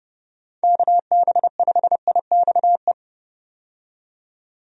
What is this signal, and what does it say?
Morse code, heard as a keyed tone, about 30 wpm, 700 Hz. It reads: KB5IXE
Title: USA Call Signs - 6 Characters - 30wpm